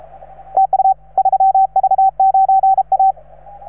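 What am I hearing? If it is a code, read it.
Morse code, heard as a keyed tone, about 35 wpm, 770 Hz, with radio noise. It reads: TU 3V9A
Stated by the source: Morse from station 3V9A.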